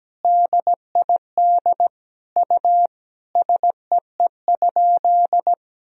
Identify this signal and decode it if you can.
Morse code, heard as a keyed tone, about 17 wpm, 700 Hz. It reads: DID U SEE?
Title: Sentences from Top 100 Words and Abbreviations 17wpm